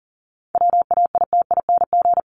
Morse code, heard as a keyed tone, about 37 wpm, 700 Hz. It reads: WAITING
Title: ING Suffix - Top 100 Words with Base Words 40wpm